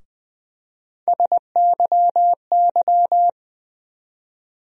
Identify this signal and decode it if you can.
Morse code, heard as a keyed tone, about 20 wpm, 700 Hz. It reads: SYY
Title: USA Call Sign Postfixes - 3 Letters 20wpm